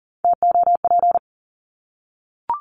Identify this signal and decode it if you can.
Morse code, heard as a keyed tone, about 40 wpm, 700 Hz. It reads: TOP
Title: Top 500 Words Repeated 40wpm